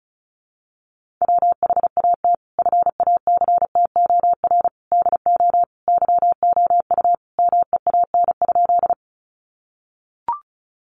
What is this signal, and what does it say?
Morse code, heard as a keyed tone, about 34 wpm, 700 Hz. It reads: WHAT FACTOR DO YOU MEAN?
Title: Sentences from Top 1300 Words 35wpm